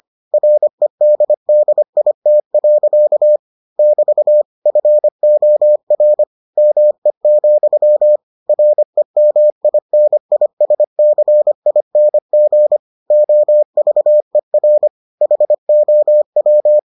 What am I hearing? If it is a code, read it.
Morse code, heard as a keyed tone, about 25 wpm, 600 Hz. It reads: REDDIT. <BT> FOR ME, REMINISCING OVER HOW